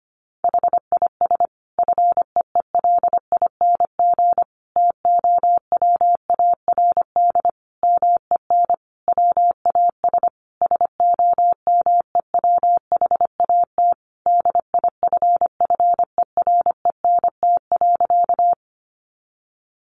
Morse code, heard as a keyed tone, about 25 wpm, 700 Hz. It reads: HIS FEELING TOWARD MEN WAS SOMEWHAT DIFFERENT.
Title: Sentences from Top 1300 Words 25wpm